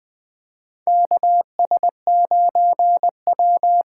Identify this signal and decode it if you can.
Morse code, heard as a keyed tone, about 21 wpm, 700 Hz. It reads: KS9W
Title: K1USN SST QSO Practice - 20wpm